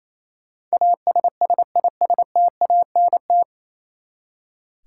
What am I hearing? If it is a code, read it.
Morse code, heard as a keyed tone, about 28 wpm, 700 Hz. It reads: ASSISTANT
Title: Practice QSO - Exchange 3 - More Info - 28wpm